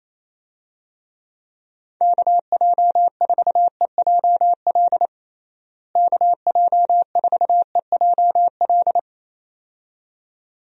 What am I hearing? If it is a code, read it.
Morse code, heard as a keyed tone, about 27 wpm, 700 Hz. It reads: KJ4EJL KJ4EJL
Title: USA Call Signs - 6 Characters - Repeated 2x - 28wpm